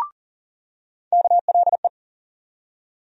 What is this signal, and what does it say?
Morse code, heard as a keyed tone, about 40 wpm, 700 Hz. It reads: KLE
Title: USA Call Sign Postfixes - 3 Letters 40wpm